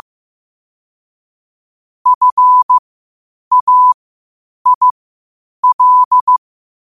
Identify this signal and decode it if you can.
Morse code, heard as a keyed tone, about 15 wpm, spaced as for 8 wpm, 1000 Hz. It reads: FAIL
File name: qotdfiles / 8wpm